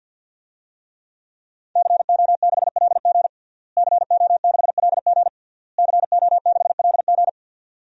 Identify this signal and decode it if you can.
Morse code, heard as a keyed tone, about 50 wpm, 700 Hz. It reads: CO6LG CO6LG CO6LG
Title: DXCC-100 Call Signs - 5 Characters - Repeated 3x - 50wpm